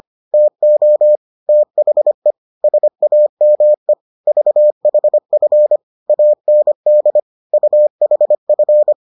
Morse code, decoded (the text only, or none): TO THE SAME VHF AND UHF